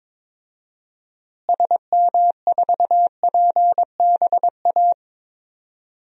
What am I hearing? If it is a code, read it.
Morse code, heard as a keyed tone, about 22 wpm, 700 Hz. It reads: SM4PBA
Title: DXCC-100 Call Signs - 6 Characters - Repeated 3x - 22wpm